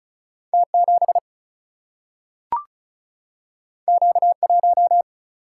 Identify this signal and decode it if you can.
Morse code, heard as a keyed tone, about 36 wpm, 700 Hz. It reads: T 7 Q 1
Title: Single-Letter Single-Number 35wpm